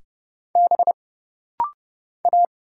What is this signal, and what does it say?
Morse code, heard as a keyed tone, about 30 wpm, 700 Hz. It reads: B A
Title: Single Letter-Number - Rapid-Fire 30wpm